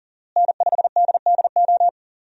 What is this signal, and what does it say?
Morse code, heard as a keyed tone, about 38 wpm, 700 Hz. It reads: N5DDO
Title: USA Call Signs - 5 Characters - Repeated 3x - 40wpm